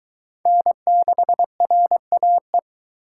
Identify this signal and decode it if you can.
Morse code, heard as a keyed tone, about 22 wpm, 700 Hz. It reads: N6RAE